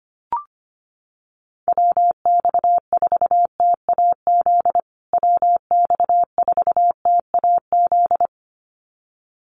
Morse code, heard as a keyed tone, about 25 wpm, 700 Hz. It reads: WX4TAZ WX4TAZ
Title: USA Call Signs Repeated 2x 25wpm